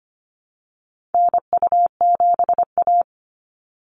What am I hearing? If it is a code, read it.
Morse code, heard as a keyed tone, about 25 wpm, 700 Hz. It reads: NU7A
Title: NA SOTA Activator Callsigns 25wpm